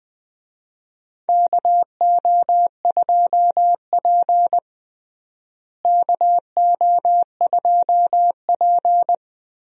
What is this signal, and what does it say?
Morse code, heard as a keyed tone, about 20 wpm, 700 Hz. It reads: KO2P KO2P
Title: USA Call Signs - 4 Characters - Repeated 2x - 20wpm